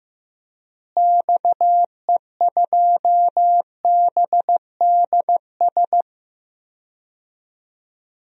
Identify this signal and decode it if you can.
Morse code, heard as a keyed tone, about 15 wpm, 700 Hz. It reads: XE2BDS
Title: DXCC-100 Call Signs - 6 Characters - Repeated 3x - 15wpm